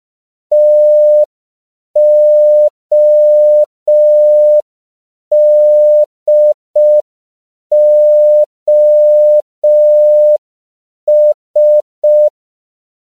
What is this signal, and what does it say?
Morse code, heard as a keyed tone, about 5 wpm, 600 Hz. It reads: TODOS